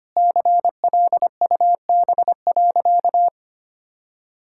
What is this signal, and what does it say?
Morse code, heard as a keyed tone, about 25 wpm, 700 Hz. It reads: CLUB.
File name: practice / 25wpm / All-Age Appropriate Jokes - 25wpm